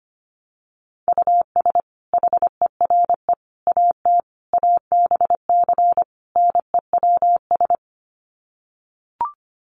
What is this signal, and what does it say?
Morse code, heard as a keyed tone, about 25 wpm, 700 Hz. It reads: US HERE AT ABC NEWS
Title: Sets of 5 Words 25wpm